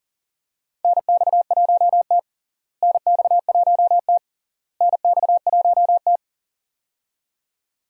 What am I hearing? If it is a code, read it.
Morse code, heard as a keyed tone, about 40 wpm, 700 Hz. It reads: NX1T NX1T NX1T
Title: USA Call Signs - 4 Characters - Repeated 3x - 40wpm